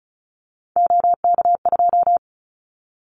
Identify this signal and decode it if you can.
Morse code, heard as a keyed tone, about 34 wpm, 700 Hz. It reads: OK2
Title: Callsign Prefixes for Top 25 DXCC - 35wpm